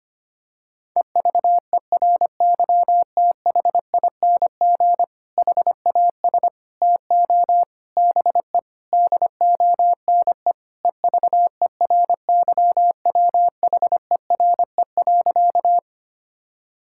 Morse code, heard as a keyed tone, about 25 wpm, 700 Hz. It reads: EVERYTHING HAS TO BE DONE EVERYWHERE.
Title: Sentences from Top 1600 Words 25wpm